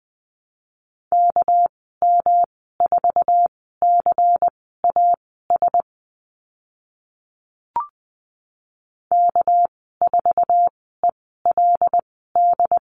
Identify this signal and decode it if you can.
Morse code, heard as a keyed tone, about 20 wpm, spaced as for 14 wpm, 700 Hz. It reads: KM4CAS K4ELD
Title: 245 Lesson 39g - Letter z - Call Signs 20wpm